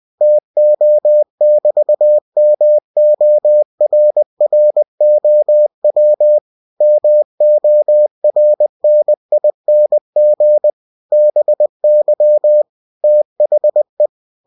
Morse code, heard as a keyed tone, about 20 wpm, 600 Hz. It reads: TO<BT>MORROW MORNING BY THE
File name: cw-books / winnie-the-pooh / winnie-20wpm-page-0038